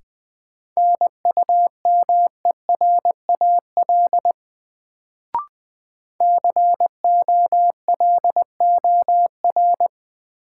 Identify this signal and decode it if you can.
Morse code, heard as a keyed tone, about 20 wpm, 700 Hz. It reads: NUMERAL COLOR
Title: Sentences from Top 1000 Words - Review Part 2 - New Word Drill - 20wpm